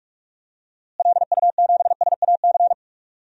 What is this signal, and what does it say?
Morse code, heard as a keyed tone, about 41 wpm, 700 Hz. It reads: LU7SAC
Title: DXCC-100 Call Signs - 6 Characters - Repeated 3x - 45wpm